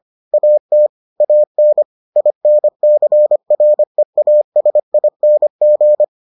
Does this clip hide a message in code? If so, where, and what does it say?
Yes: Morse code from 0.3 to 6.0 s, heard as a keyed tone, about 25 wpm, 600 Hz. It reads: AT AN INCREASING